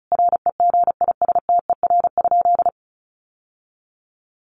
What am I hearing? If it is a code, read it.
Morse code, heard as a keyed tone, about 34 wpm, 700 Hz. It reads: REGISTER?
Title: Sentences from Top 1200 Words 35wpm